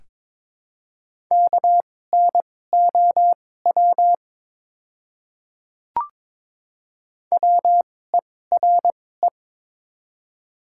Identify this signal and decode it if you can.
Morse code, heard as a keyed tone, about 22 wpm, spaced as for 16 wpm, 700 Hz. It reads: KNOW WERE
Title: 2x Character Spacing - Top 300 Words - 22wpm